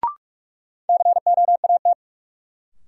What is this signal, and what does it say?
Morse code, heard as a keyed tone, about 44 wpm, 700 Hz. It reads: COAT
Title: Sentences from Top 1600 Words - New Word Drill Alt - 45wpm